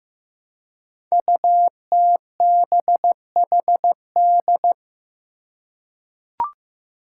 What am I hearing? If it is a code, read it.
Morse code, heard as a keyed tone, about 15 wpm, 700 Hz. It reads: UTBHD
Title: Five Challenging Letter-Number Combinations 15wpm